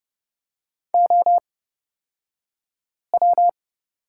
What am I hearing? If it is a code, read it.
Morse code, heard as a keyed tone, about 30 wpm, spaced as for 5 wpm, 700 Hz. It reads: OW
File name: Lesson 37d - Letter x - Two Characters 30wpm